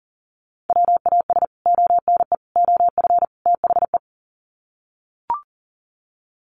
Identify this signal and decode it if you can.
Morse code, heard as a keyed tone, about 40 wpm, 700 Hz. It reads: WAS ONE OF THE
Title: Sets of 4 Words 40wpm